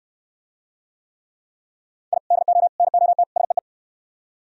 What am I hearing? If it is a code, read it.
Morse code, heard as a keyed tone, about 63 wpm, 700 Hz. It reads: I DO NOT SEE